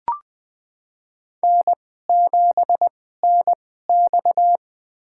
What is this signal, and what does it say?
Morse code, heard as a keyed tone, about 20 wpm, spaced as for 14 wpm, 700 Hz. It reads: N7NX
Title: USA Call Signs - 4 Characters - 2x Character Spacing - 20wpm